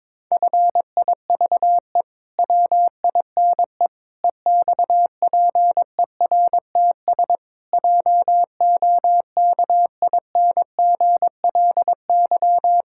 Morse code, heard as a keyed tone, about 22 wpm, 700 Hz. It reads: FIVE WINE EXPERTS JOKINGLY
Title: Pangram Challenge v2 - 22wpm